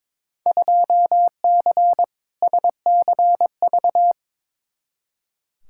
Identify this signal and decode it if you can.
Morse code, heard as a keyed tone, about 22 wpm, 700 Hz. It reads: 2C SCV
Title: Field Day Exchanges Encore - 22wpm